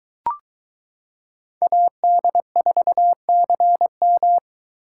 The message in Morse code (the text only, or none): AD4CM